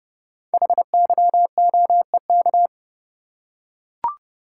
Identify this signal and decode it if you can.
Morse code, heard as a keyed tone, about 28 wpm, 690 Hz. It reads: HYOEK
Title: Code Groups - 5 Letters - 30wpm